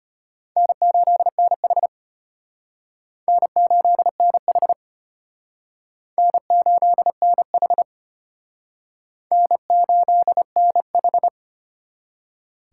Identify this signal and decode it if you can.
Morse code, heard as a keyed tone, about 32 wpm, 700 Hz. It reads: N8NH N8NH N8NH N8NH